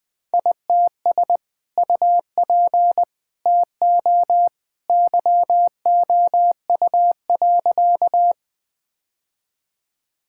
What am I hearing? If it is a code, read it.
Morse code, heard as a keyed tone, about 21 wpm, 700 Hz. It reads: ITS UP TO YOU.
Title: Sentences with 2-Word Phrases - Spoken Beginning - 20wpm